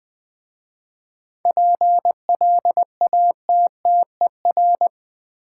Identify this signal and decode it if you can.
Morse code, heard as a keyed tone, about 20 wpm, 700 Hz. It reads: PLATTER
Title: Easy Words with Hints - 20wpm